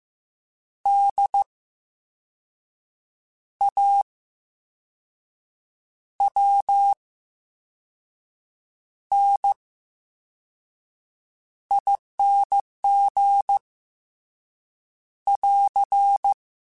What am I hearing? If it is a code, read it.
Morse code, heard as a keyed tone, about 15 wpm, spaced as for 5 wpm, 800 Hz. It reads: DAWNING <AR>
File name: Special INGs / Special INGs-mp3s / Slow ing @13